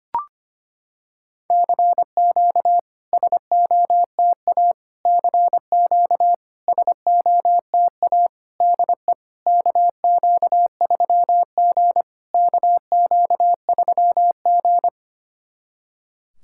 Morse code, heard as a keyed tone, about 25 wpm, 690 Hz. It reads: CQ SOTA CQ SOTA DE KQ3G KQ3G
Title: SOTA QSO Practice 25wpm